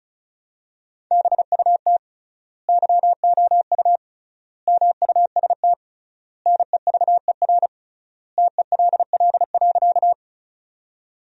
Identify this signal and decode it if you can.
Morse code, heard as a keyed tone, about 36 wpm, 700 Hz. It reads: BUT YOU MUST NEVER TELL.
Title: 3x Word-Spacing - Sentences from Top 300 Words - 35wpm